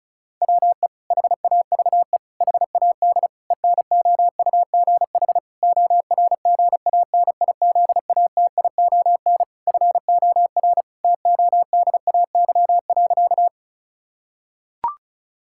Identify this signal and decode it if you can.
Morse code, heard as a keyed tone, about 34 wpm, 700 Hz. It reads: WE HAVE HAD ENOUGH ORGANIZATION FOR TODAY.
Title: Sentences from Top 1000 Words - Review Part 5 - 35wpm